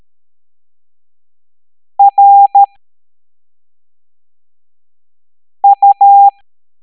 Morse code, heard as a keyed tone, about 13 wpm, 800 Hz. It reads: R U